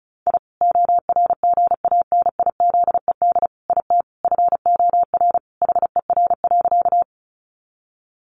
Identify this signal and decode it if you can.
Morse code, heard as a keyed tone, about 34 wpm, 700 Hz. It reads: I ORGANIZED IT FOR HER.
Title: Sentences from Top 1300 Words 35wpm